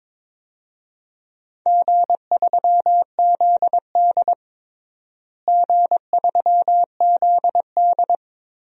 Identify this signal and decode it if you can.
Morse code, heard as a keyed tone, about 22 wpm, 700 Hz. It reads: G3ZD G3ZD
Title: DXCC-100 Call Signs - 4 Characters - Repeated 3x - 22wpm